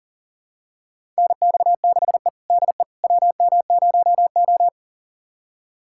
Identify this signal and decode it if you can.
Morse code, heard as a keyed tone, about 40 wpm, 700 Hz. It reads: NX6E DE WM0O
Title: USA Callsign DE Callsign - 4 Characters - 40wpm